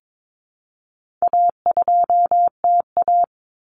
Morse code, heard as a keyed tone, about 22 wpm, 700 Hz. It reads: A2TA
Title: Common Cut Numbers - 4-Digit 22wpm